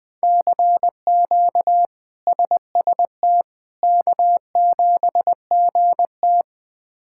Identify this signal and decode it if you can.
Morse code, heard as a keyed tone, about 20 wpm, 700 Hz. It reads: CQ SST K7GT
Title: K1USN SST QSO Practice - 20wpm